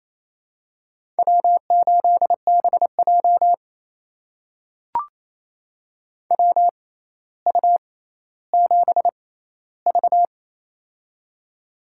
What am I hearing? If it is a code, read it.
Morse code, heard as a keyed tone, about 28 wpm, spaced as for 14 wpm, 700 Hz. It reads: W8BJ WU7V
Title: USA Call Signs - 6x Character Spacing - 28wpm